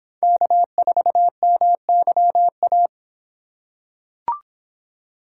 Morse code, heard as a keyed tone, about 26 wpm, 700 Hz. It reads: K4MYA